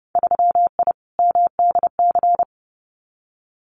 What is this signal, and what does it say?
Morse code, heard as a keyed tone, about 30 wpm, 700 Hz. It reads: 3I MDC
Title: Winter Field Day Exchanges Encore 30wpm